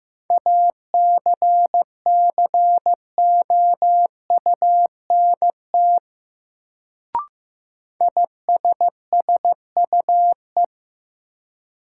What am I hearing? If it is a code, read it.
Morse code, heard as a keyed tone, about 15 wpm, 700 Hz. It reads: ACCOUNT ISSUE
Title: Sentences from Top 900 Words - New Word Drill - 15wpm